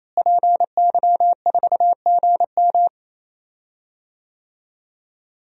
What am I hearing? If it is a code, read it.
Morse code, heard as a keyed tone, about 28 wpm, 700 Hz. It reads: PY4GM
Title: DXCC-100 Call Signs - 5 Characters - Repeated 3x - 28wpm